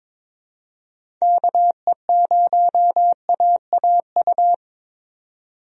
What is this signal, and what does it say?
Morse code, heard as a keyed tone, about 22 wpm, 700 Hz. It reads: KE0AAU
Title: USA Call Signs - 6 Characters - 22wpm